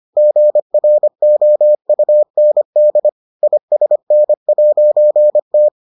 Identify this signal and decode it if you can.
Morse code, heard as a keyed tone, about 25 wpm, 600 Hz. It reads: GROUND ISN'T